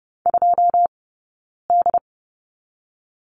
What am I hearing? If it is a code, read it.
Morse code, heard as a keyed tone, about 30 wpm, 700 Hz. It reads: 2 D
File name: 092 Lesson 16d - Letter c - Two Characters 30wpm